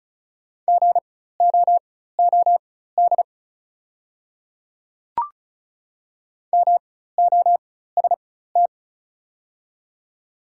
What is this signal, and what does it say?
Morse code, heard as a keyed tone, about 36 wpm, spaced as for 16 wpm, 700 Hz. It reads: GOOD MOST
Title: 4x Character Spacing - Top 200 Words - 35wpm